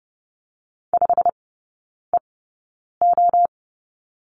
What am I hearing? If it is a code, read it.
Morse code, heard as a keyed tone, about 30 wpm, spaced as for 9 wpm, 700 Hz. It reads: 5EO
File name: course / 30:10wpm / 087 Lesson 15e - Number 5 - Three Characters 30wpm